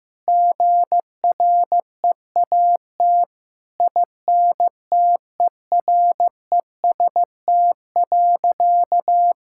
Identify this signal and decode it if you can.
Morse code, heard as a keyed tone, about 15 wpm, 700 Hz. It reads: GREAT INTEREST.
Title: Sentences from Top 1000 Words - Review Part 6 - 15wpm